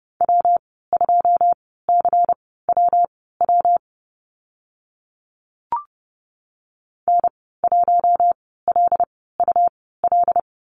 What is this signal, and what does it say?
Morse code, heard as a keyed tone, about 28 wpm, spaced as for 17 wpm, 700 Hz. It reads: W2CWW N1LUL
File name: Lesson 19g - Letter w - Call Signs 30wpm